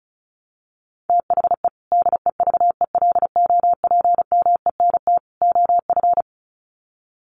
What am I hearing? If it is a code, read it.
Morse code, heard as a keyed tone, about 34 wpm, 700 Hz. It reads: THE DEVELOPMENT OF
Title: Sets of 3 Words 35wpm